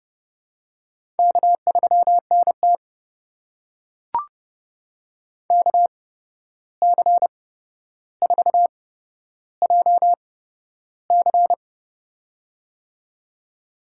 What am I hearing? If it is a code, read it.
Morse code, heard as a keyed tone, about 31 wpm, spaced as for 12 wpm, 700 Hz. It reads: K3NT KC4JC
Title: USA Call Signs - 8x Character Spacing - 30wpm